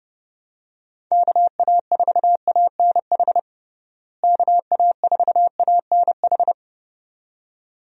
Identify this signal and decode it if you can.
Morse code, heard as a keyed tone, about 31 wpm, 700 Hz. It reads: KA4ANH KA4ANH
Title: USA Call Signs - 6 Characters - Repeated 2x - 30wpm